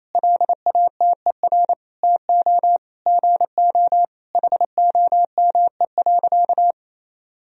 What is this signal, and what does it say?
Morse code, heard as a keyed tone, about 28 wpm, 700 Hz. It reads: LATER TO GO HOME.
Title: Sentences from Top 1000 Words - Review Part 4 - 28wpm